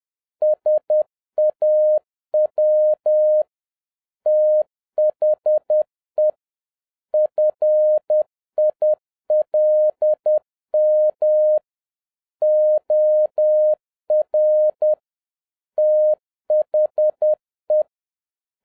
Morse code, heard as a keyed tone, about 10 wpm, 610 Hz. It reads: SAW THE FILM OR THE